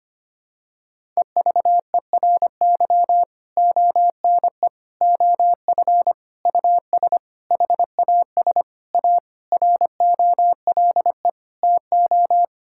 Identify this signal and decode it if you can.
Morse code, heard as a keyed tone, about 25 wpm, 700 Hz. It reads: EVERY ONE OF US HAS A ROLE TO
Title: Sentences from Top 1000 Words - Review Part 6 - 25wpm